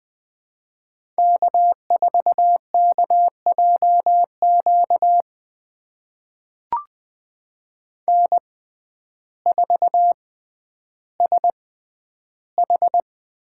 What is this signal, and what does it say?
Morse code, heard as a keyed tone, about 20 wpm, spaced as for 11 wpm, 700 Hz. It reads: K4KJQ N4SH
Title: USA Call Signs - 5 Characters - 6x Character Spacing - 20wpm